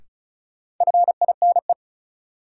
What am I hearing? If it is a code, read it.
Morse code, heard as a keyed tone, about 34 wpm, 700 Hz. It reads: FINE
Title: Sentences from Top 500 Words - New Word Drill - 35wpm